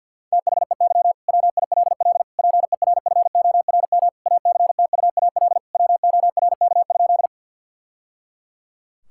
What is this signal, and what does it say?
Morse code, heard as a keyed tone, about 50 wpm, 700 Hz. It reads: THEY WILL PERFORM ACTUAL WORK?
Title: Sentences from Top 1000 Words - Review Part 5 - 50wpm